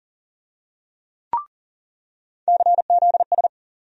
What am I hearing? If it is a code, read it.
Morse code, heard as a keyed tone, about 40 wpm, 700 Hz. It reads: CZS
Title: USA Call Sign Postfixes - 3 Letters 40wpm